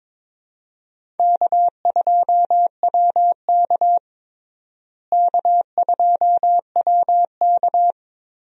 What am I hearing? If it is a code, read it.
Morse code, heard as a keyed tone, about 22 wpm, 700 Hz. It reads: K2WK K2WK
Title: USA Call Signs - 4 Characters - Repeated 3x - 22wpm